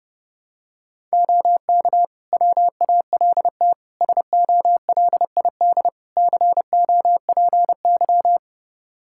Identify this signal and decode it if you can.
Morse code, heard as a keyed tone, about 32 wpm, 700 Hz. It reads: OK WALT SOLID COPY
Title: Practice QSO - Exchange 3 - More Info - 30wpm